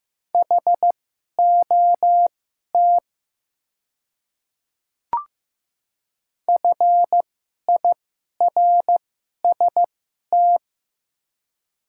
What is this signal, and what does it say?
Morse code, heard as a keyed tone, about 15 wpm, spaced as for 11 wpm, 700 Hz. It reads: HOT FIRST
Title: 2x Character Spacing - Top 100 Words - 15wpm